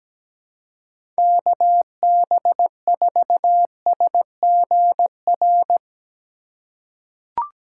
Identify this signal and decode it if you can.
Morse code, heard as a keyed tone, about 17 wpm, 700 Hz. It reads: KB4SGR
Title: USA Call Signs - 6 Characters - 3x Character Spacing - 17wpm